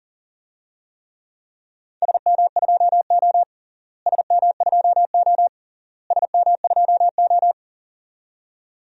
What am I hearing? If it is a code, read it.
Morse code, heard as a keyed tone, about 37 wpm, 700 Hz. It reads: SM2O SM2O SM2O
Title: DXCC-100 Call Signs - 4 Characters - Repeated 3x - 40wpm